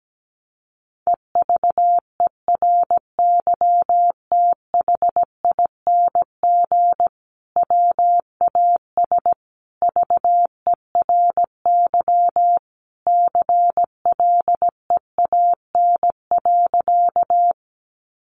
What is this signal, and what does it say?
Morse code, heard as a keyed tone, about 17 wpm, 700 Hz. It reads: EVERYTHING WAS VERY CLEAN.
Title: Sentences from Top 1100 Words 17wpm